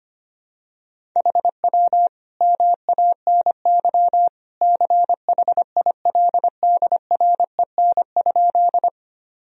Morse code, heard as a keyed tone, about 25 wpm, 700 Hz. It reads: HW MANY CHILDREN?
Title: Sentences from Top 300 Words and Abbreviations 25wpm